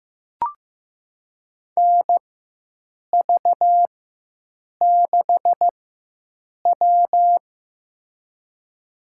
Morse code, heard as a keyed tone, about 16 wpm, spaced as for 7 wpm, 700 Hz. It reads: NV6W
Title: USA Call Signs - 4 Characters - 4x Character Spacing - 15wpm